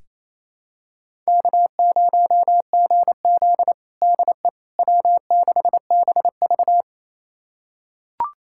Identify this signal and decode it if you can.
Morse code, heard as a keyed tone, about 28 wpm, 700 Hz. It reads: K0GZ DE W6BV
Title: USA Callsign DE Callsign - 4 Characters - 28wpm